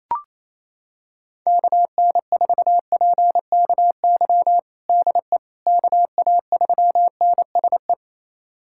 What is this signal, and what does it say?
Morse code, heard as a keyed tone, about 27 wpm, 700 Hz. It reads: KN4PKY DE KA3NSE
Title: USA Callsign DE Callsign - 6 Characters - 28wpm